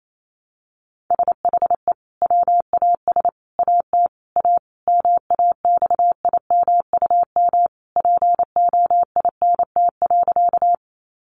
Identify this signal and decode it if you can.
Morse code, heard as a keyed tone, about 28 wpm, 700 Hz. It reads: SHE WAS AT A MAXIMUM POINT.
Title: Sentences from Top 1300 Words 28wpm